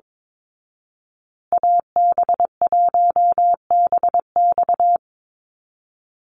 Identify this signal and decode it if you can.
Morse code, heard as a keyed tone, about 22 wpm, 700 Hz. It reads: AB1BX
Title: Call Signs Repeated 3x - 22wpm